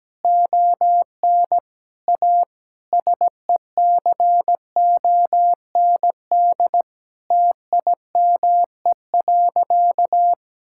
Morse code, heard as a keyed tone, about 17 wpm, 700 Hz. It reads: ON A SECOND TIME.
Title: Sentences from Top 1000 Words - Review Part 5 - 17wpm